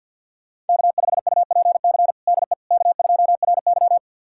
Morse code, heard as a keyed tone, about 50 wpm, 700 Hz. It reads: K5UPC DE K2RY
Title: USA Callsign DE Callsign 50wpm